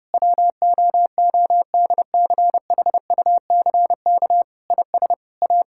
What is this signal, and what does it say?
Morse code, heard as a keyed tone, about 32 wpm, 700 Hz. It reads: WOODCHUCK IS A
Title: Pangram Challenge v2 - 30wpm